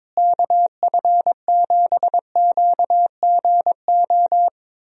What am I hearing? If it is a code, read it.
Morse code, heard as a keyed tone, about 22 wpm, 700 Hz. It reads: KF7QGO
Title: USA Call Signs - 6 Characters - Repeated 2x - 22wpm